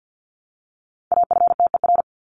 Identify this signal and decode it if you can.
Morse code, heard as a keyed tone, about 50 wpm, 700 Hz. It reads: AFTER
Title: Top 300 Words Repeated 50wpm